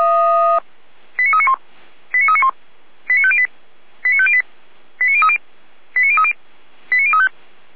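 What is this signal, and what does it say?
Five-tone selective call (ZVEI), heard as quick runs of five tones, tones of about 70 ms. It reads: 89391 89391 89508 89508 89030 89030 89035
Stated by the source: ZVEI five-tone selective call received on SDR.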